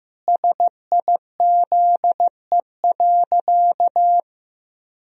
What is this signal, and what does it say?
Morse code, heard as a keyed tone, about 16 wpm, 700 Hz. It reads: SIZE.
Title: Sentences from Top 1000 Words - Review Part 2 - 15wpm